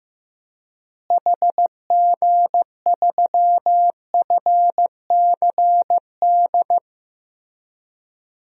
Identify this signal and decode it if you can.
Morse code, heard as a keyed tone, about 15 wpm, 700 Hz. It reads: HG3FCD
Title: DXCC-100 Call Signs - 6 Characters - Repeated 2x - 15wpm